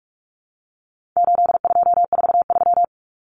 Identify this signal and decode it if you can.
Morse code, heard as a keyed tone, about 44 wpm, 700 Hz. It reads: 8243
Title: Common Cut Numbers - 4-Digit 45wpm